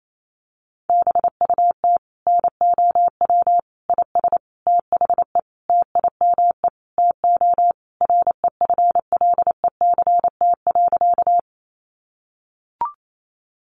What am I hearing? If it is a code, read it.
Morse code, heard as a keyed tone, about 28 wpm, 700 Hz. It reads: BUT NOW IS THE TIME TO REFLECT.
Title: Sentences from Top 1500 Words 28wpm